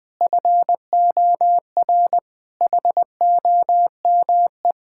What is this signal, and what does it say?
Morse code, heard as a keyed tone, about 20 wpm, 700 Hz. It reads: FOR HOME
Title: Difficult Sentences with Hints - 20wpm